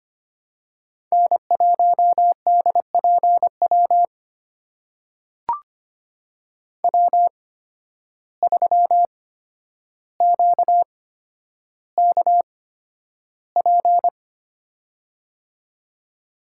Morse code, heard as a keyed tone, about 25 wpm, spaced as for 11 wpm, 700 Hz. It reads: N1DPW W3QKP
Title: USA Call Signs - 5 Characters - 8x Character Spacing - 25wpm